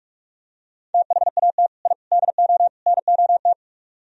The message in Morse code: THAT I DO NOT

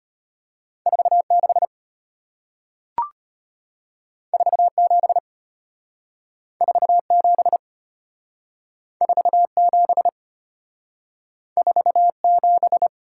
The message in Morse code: 46 47 47 47 47

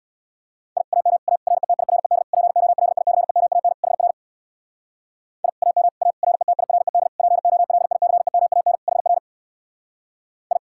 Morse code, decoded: I AM A RETIRED POLICEMAN HR I AM A RETIRED POLICEMAN HR I

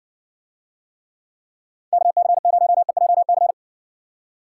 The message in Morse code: KC0EJC

